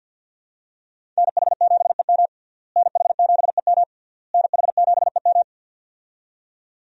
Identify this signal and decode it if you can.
Morse code, heard as a keyed tone, about 52 wpm, 700 Hz. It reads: NH7EM NH7EM NH7EM
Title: USA Call Signs - 5 Characters - Repeated 3x - 50wpm